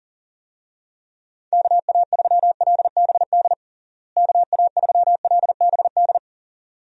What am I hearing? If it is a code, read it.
Morse code, heard as a keyed tone, about 37 wpm, 700 Hz. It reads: KA3LBD KA3LBD
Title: USA Call Signs - 6 Characters - Repeated 2x - 40wpm